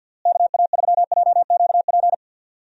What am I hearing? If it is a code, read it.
Morse code, heard as a keyed tone, about 50 wpm, 700 Hz. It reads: KA3JQP
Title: USA Call Signs - 6 Characters - Repeated 3x - 50wpm